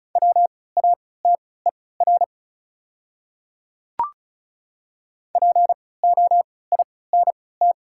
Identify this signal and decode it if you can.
Morse code, heard as a keyed tone, about 34 wpm, spaced as for 20 wpm, 700 Hz. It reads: WATER POINT
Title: 3x Character Spacing - Top 200 Words - 35wpm